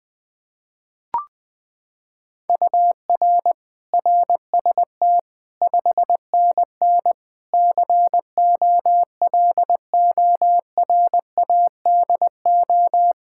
Morse code, heard as a keyed tone, about 21 wpm, 700 Hz. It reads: UR RST 5NN COLORADO
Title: Practice QSO - Exchange 1 - Beginning - 20wpm